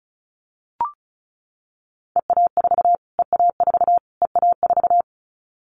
Morse code, heard as a keyed tone, about 34 wpm, 700 Hz. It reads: EA4 EA4 EA4
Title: Callsign Prefixes for Top 25 DXCC - Repeated 3x - 35wpm